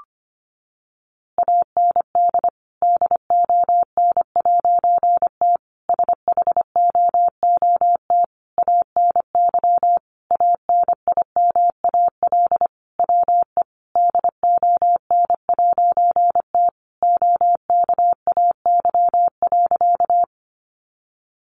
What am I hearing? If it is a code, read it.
Morse code, heard as a keyed tone, about 25 wpm, 700 Hz. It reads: AND DON'T SHOOT ANY ANIMAL WE DON'T OKAY.
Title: A Sound Of Thunder 25wpm